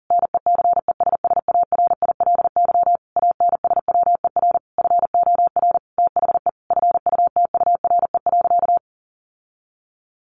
Morse code, heard as a keyed tone, about 40 wpm, 700 Hz. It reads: NECESSARILY ANSWER FOR THE FUTURE.
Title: Sentences from Top 1400 Words 40wpm